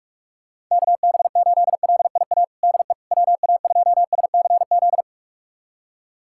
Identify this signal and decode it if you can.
Morse code, heard as a keyed tone, about 44 wpm, 700 Hz. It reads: KB8LIA DE WA2SCZ
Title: USA Callsign DE Callsign - 6 Characters - 45wpm